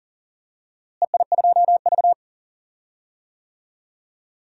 Morse code, heard as a keyed tone, about 40 wpm, 700 Hz. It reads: EI2V
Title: DXCC-100 Call Signs - 4 Characters - Repeated 3x - 40wpm